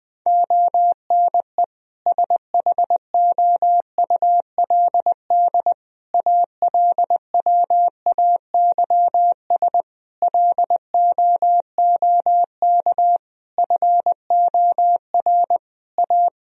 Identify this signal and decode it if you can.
Morse code, heard as a keyed tone, about 20 wpm, 700 Hz. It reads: ONE SHOULD ALWAYS LOOK FOR A
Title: Sentences from Top 1000 Words - Review Part 4 - 20wpm